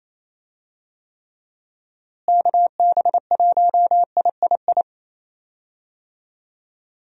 Morse code, heard as a keyed tone, about 28 wpm, 700 Hz. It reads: KB1III
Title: USA Call Signs - 6 Characters - Repeated 3x - 28wpm